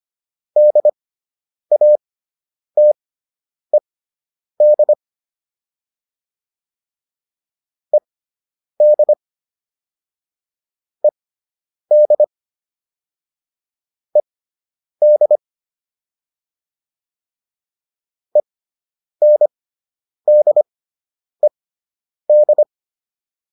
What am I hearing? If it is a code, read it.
Morse code, heard as a keyed tone, about 25 wpm, spaced as for 9 wpm, 600 Hz. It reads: DATED ED ED ED ENDED
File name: ED5 09